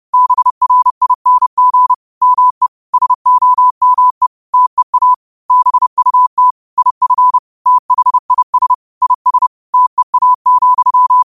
Morse code, heard as a keyed tone, about 30 wpm, 1000 Hz. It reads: BRING ME SOME TEA BUT IF THIS IS TEA,